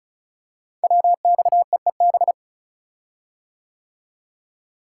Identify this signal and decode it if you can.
Morse code, heard as a keyed tone, about 36 wpm, 690 Hz. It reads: WXEEB
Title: Code Groups - 5 Letters - 35wpm